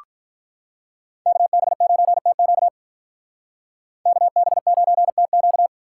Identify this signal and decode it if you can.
Morse code, heard as a keyed tone, about 44 wpm, 700 Hz. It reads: KB9TQ KB9TQ